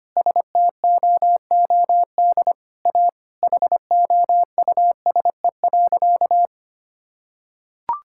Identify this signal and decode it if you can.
Morse code, heard as a keyed tone, about 25 wpm, 700 Hz. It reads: STOOD A HOUSE.
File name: practice / 25wpm / Sentences with 4-Word Phrases - 25wpm